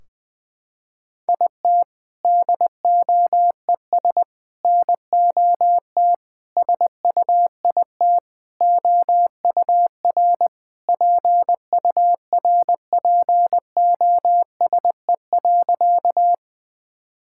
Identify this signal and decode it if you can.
Morse code, heard as a keyed tone, about 19 wpm, 700 Hz. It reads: IT DOES NOT SUIT OUR PURPOSE.
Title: Sentences from Top 1600 Words 20wpm